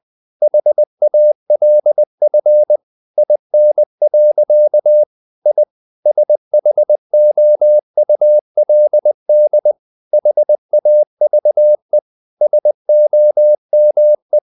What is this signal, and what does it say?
Morse code, heard as a keyed tone, about 20 wpm, 600 Hz. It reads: HALF IN. I SHOULD HAVE SOME
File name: the old man and the sea part 2 20wpm